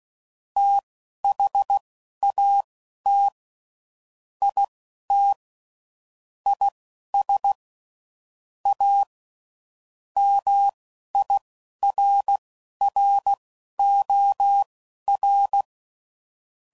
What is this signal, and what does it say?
Morse code, heard as a keyed tone, about 16 wpm, spaced as for 11 wpm, 800 Hz. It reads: THAT IT IS A MIRROR
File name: CW Code Groups @10WPM / Technical Subjects / OR Gate